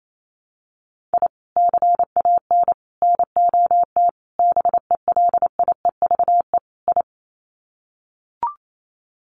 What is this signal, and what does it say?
Morse code, heard as a keyed tone, about 28 wpm, 700 Hz. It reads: I CAN NOT BELIEVE I
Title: Sets of 5 Words Encore2 28wpm